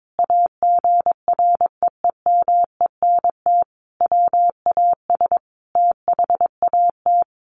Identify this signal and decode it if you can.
Morse code, heard as a keyed tone, about 22 wpm, 700 Hz. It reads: AGREEMENT WAS THAT
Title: Sentences from Top 1500 Words 22wpm